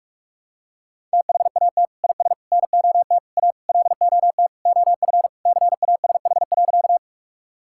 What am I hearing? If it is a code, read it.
Morse code, heard as a keyed tone, about 41 wpm, 700 Hz. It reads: THAT IS NOT A LOT OF CASH.
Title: Sentences from Top 1000 Words - Review Part 5 - 45wpm